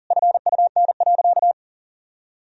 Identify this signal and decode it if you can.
Morse code, heard as a keyed tone, about 37 wpm, 700 Hz. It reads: FUN.
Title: Sentences from Top 1000 Words - Review Part 4 - 40wpm